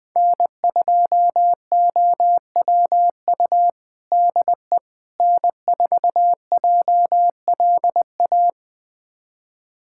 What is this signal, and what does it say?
Morse code, heard as a keyed tone, about 20 wpm, 700 Hz. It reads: N2OWU DE N4JLA
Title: USA Callsign DE Callsign - 5 Characters - 20wpm